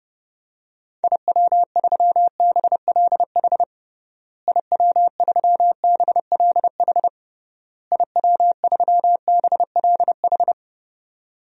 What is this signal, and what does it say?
Morse code, heard as a keyed tone, about 30 wpm, 700 Hz. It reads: IW3BLH IW3BLH IW3BLH
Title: DXCC-100 Call Signs - 6 Characters - Repeated 3x - 30wpm